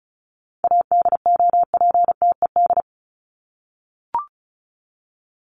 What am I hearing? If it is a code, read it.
Morse code, heard as a keyed tone, about 34 wpm, 700 Hz. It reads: ADOPTED
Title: ED Suffix - Top 100 Words with Base Words 35wpm